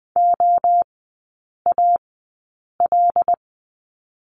Morse code, heard as a keyed tone, about 20 wpm, spaced as for 8 wpm, 700 Hz. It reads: OAL
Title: 114 Lesson 19e - Letter w - Three Characters 20wpm